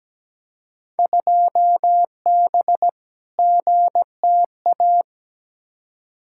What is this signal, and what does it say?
Morse code, heard as a keyed tone, about 17 wpm, 700 Hz. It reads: 2B GTA
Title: Field Day Exchanges Encore - 17wpm